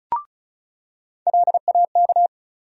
Morse code, heard as a keyed tone, about 34 wpm, 700 Hz. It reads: LAK
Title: USA Call Sign Postfixes - 3 Letters 35wpm